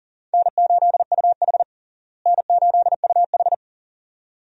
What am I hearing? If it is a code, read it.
Morse code, heard as a keyed tone, about 40 wpm, 700 Hz. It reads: N8UH N8UH
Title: USA Call Signs - 4 Characters - Repeated 2x - 40wpm